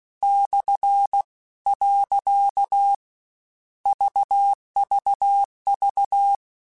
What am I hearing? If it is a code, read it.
Morse code, heard as a keyed tone, about 16 wpm, 800 Hz. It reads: / . VVV
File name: Misc / Misc-mp3s / Punct @15